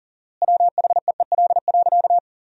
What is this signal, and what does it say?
Morse code, heard as a keyed tone, about 37 wpm, 700 Hz. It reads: WHEEL.
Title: 4x Word-Spacing - Sentences from Top 500 Words - 40wpm